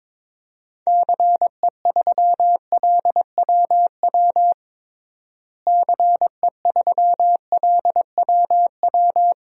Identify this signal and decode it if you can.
Morse code, heard as a keyed tone, about 22 wpm, 700 Hz. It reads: CE3LWW CE3LWW
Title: DXCC-100 Call Signs - 6 Characters - Repeated 3x - 22wpm